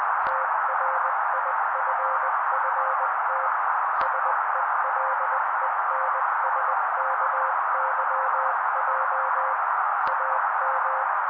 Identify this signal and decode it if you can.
Morse code, heard as a keyed tone, about 20 wpm, 530 Hz. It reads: TRIFFT SELENSKYJ AM
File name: daily / 18wpm noise